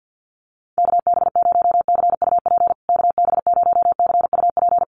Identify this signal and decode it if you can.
Morse code, heard as a keyed tone, about 50 wpm, 700 Hz. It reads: KB0CUP KB0CUP
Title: USA Call Signs Repeated 3x 50wpm